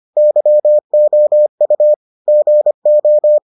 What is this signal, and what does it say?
Morse code, heard as a keyed tone, about 25 wpm, 600 Hz. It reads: YOU GO